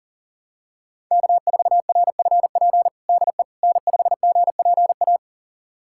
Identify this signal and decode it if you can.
Morse code, heard as a keyed tone, about 37 wpm, 700 Hz. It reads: K4RFP DE N5GPA
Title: USA Callsign DE Callsign - 5 Characters - 40wpm